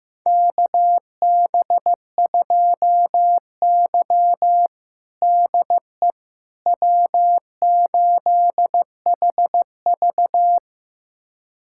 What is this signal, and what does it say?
Morse code, heard as a keyed tone, about 15 wpm, 700 Hz. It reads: KB2Y DE W8HV
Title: USA Callsign DE Callsign - 4 Characters - 15wpm